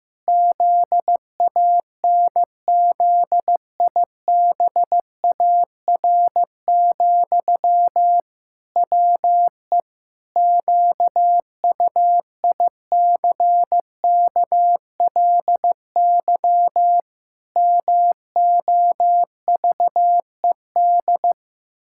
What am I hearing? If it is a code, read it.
Morse code, heard as a keyed tone, about 16 wpm, 700 Hz. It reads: ZANZIBAR, WE QUICKLY MOVED
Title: Pangram Challenge v2 - 15wpm